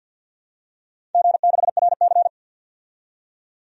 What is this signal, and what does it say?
Morse code, heard as a keyed tone, about 51 wpm, 700 Hz. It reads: G6RC
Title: DXCC-100 Call Signs - 4 Characters - 50wpm